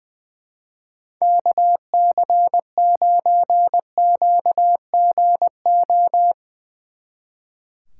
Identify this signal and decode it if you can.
Morse code, heard as a keyed tone, about 23 wpm, 700 Hz. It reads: KC9QGO
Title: Speed Racing - USA 2x3 Call Signs 20wpm